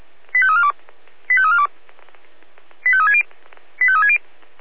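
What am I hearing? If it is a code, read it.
Five-tone selective call (ZVEI), heard as quick runs of five tones, tones of about 70 ms. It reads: 85342 85342 85360 85360
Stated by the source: ZVEI five-tone selective call received on an SDR.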